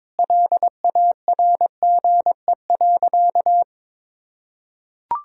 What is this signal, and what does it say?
Morse code, heard as a keyed tone, about 22 wpm, 700 Hz. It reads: LARGE.
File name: practice / 22wpm / Sentences with 3-Word Phrases - 22wpm